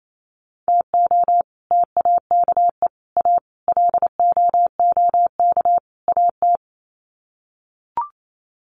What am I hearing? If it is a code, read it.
Morse code, heard as a keyed tone, about 28 wpm, 700 Hz. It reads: TO TAKE A LOOK AT
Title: Sets of 5 Words 28wpm